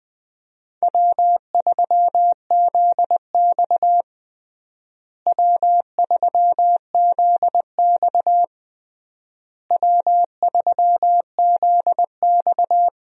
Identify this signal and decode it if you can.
Morse code, heard as a keyed tone, about 20 wpm, 700 Hz. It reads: W3ZX W3ZX W3ZX
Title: USA Call Signs - 4 Characters - Repeated 3x - 20wpm